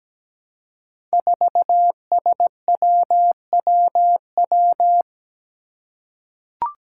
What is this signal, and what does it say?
Morse code, heard as a keyed tone, about 17 wpm, 700 Hz. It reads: 4SWWW
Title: Five Challenging Letter-Number Combinations 17wpm